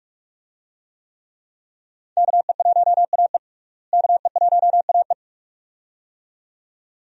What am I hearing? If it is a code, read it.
Morse code, heard as a keyed tone, about 46 wpm, 700 Hz. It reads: KE1AE KE1AE
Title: USA Call Signs - 5 Characters - Repeated 2x - 45wpm